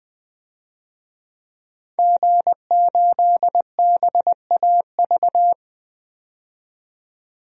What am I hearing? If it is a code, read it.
Morse code, heard as a keyed tone, about 20 wpm, 700 Hz. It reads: G8BAV
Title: DXCC-100 Call Signs - 5 Characters - 20wpm